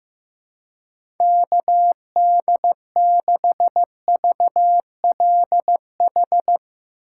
Morse code, heard as a keyed tone, about 15 wpm, 700 Hz. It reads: KD6VLH
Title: USA Call Signs - 2x Character Spacing - 15wpm